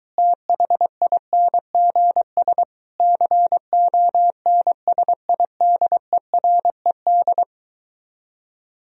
Morse code, heard as a keyed tone, about 23 wpm, 700 Hz. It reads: THINGS CONSIDERED